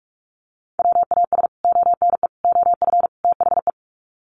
Morse code, heard as a keyed tone, about 44 wpm, 700 Hz. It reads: WAS ONE OF THE
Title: Sets of 4 Words 45wpm